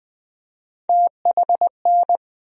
Morse code, heard as a keyed tone, about 20 wpm, 700 Hz. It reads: THN
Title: USA Call Sign Postfixes - 3 Letters 20wpm